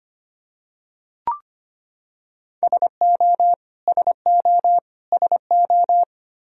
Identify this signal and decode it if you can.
Morse code, heard as a keyed tone, about 25 wpm, 700 Hz. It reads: SO SO SO
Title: Top 100 Words Repeated 25wpm